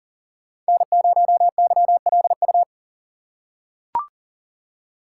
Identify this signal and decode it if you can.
Morse code, heard as a keyed tone, about 40 wpm, 700 Hz. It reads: N0YLU
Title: USA Call Signs - 5 Characters - 2x Character Spacing - 40wpm